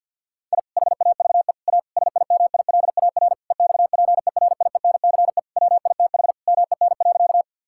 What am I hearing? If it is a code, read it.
Morse code, heard as a keyed tone, about 50 wpm, 700 Hz. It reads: I HAVE A SIMILAR EXPERIENCE WITH MEN.
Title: Sentences from Top 1000 Words - Review Part 5 - 50wpm